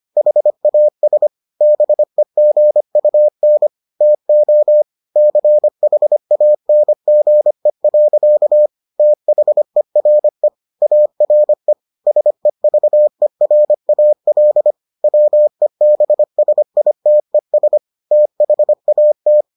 Morse code, heard as a keyed tone, about 25 wpm, 600 Hz. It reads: HAS BEGUN TO CHANGE. THERE ARE SEVERAL WEBSITES THAT